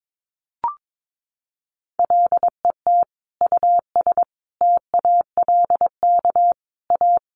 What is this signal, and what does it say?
Morse code, heard as a keyed tone, about 22 wpm, 700 Hz. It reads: LET US TALK A
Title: Sentences from Top 400 Words 22wpm